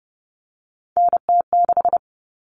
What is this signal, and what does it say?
Morse code, heard as a keyed tone, about 30 wpm, 700 Hz. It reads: NT6
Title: All Cut Numbers - 3-Digit 30wpm